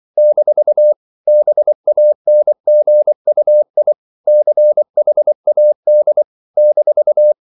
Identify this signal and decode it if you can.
Morse code, heard as a keyed tone, about 24 wpm, 600 Hz. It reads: - BANGUI CHAD -
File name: CW / Practice / NationalCapitals24